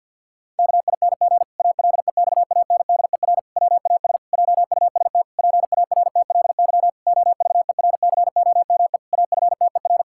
Morse code, heard as a keyed tone, about 51 wpm, 700 Hz. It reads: KING ALEXANDER WAS JUST PARTLY OVERCOME AFTER
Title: Pangram Challenge v2 - 50wpm